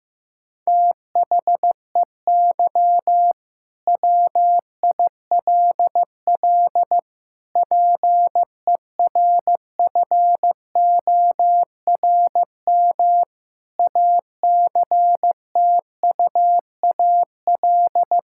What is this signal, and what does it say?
Morse code, heard as a keyed tone, about 15 wpm, 700 Hz. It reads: THEY WILL PERFORM ACTUAL
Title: Sentences from Top 1000 Words - Review Part 5 - 15wpm